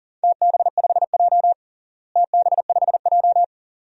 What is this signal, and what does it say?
Morse code, heard as a keyed tone, about 40 wpm, 700 Hz. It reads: TB5J TB5J
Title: DXCC-100 Call Signs - 4 Characters - Repeated 2x - 40wpm